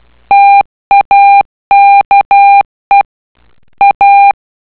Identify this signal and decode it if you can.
Morse code, heard as a keyed tone, about 12 wpm, 800 Hz. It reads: TAKE A